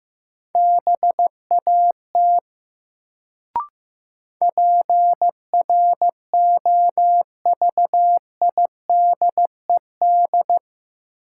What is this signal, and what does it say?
Morse code, heard as a keyed tone, about 15 wpm, 700 Hz. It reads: BAT PROVIDED
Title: Sentences from Top 1500 Words - New Word Drill - 15wpm